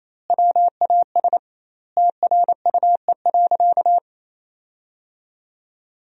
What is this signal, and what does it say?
Morse code, heard as a keyed tone, about 28 wpm, 700 Hz. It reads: WAS TRUE.
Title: 2x Word-Spacing - Sentences from Top 400 Words - 28wpm